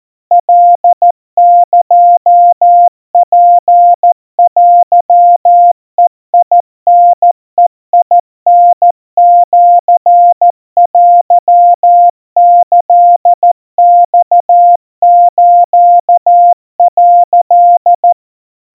Morse code (Japanese Wabun code), heard as a keyed tone, about 14 wpm, 700 Hz. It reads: カエツテベタベタシテキマス」
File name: kawa / Morse sample / kamyu 50 cpm 700 Hz Crispy